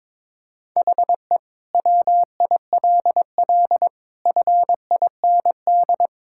Morse code, heard as a keyed tone, about 22 wpm, 700 Hz. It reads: HE WILL FIND
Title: Sentences from Top 1000 Words - Review Part 6 - 22wpm